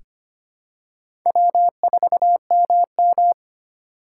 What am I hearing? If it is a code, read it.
Morse code, heard as a keyed tone, about 25 wpm, 700 Hz. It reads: W4MM
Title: USA Call Signs - 4 Characters - 2x Character Spacing - 25wpm